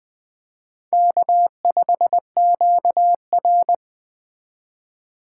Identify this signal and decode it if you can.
Morse code, heard as a keyed tone, about 20 wpm, 700 Hz. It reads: K5QR
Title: 2023 SOTA Chaser Call Signs - 20wpm